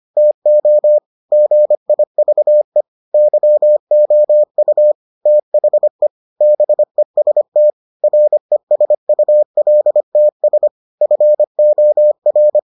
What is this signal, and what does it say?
Morse code, heard as a keyed tone, about 25 wpm, 600 Hz. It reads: TO GIVE YOU THE BEST RESULTS FOR